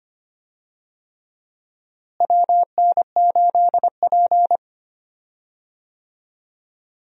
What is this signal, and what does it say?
Morse code, heard as a keyed tone, about 25 wpm, 700 Hz. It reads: WN8P
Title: USA Call Signs - 4 Characters - Repeated 3x - 25wpm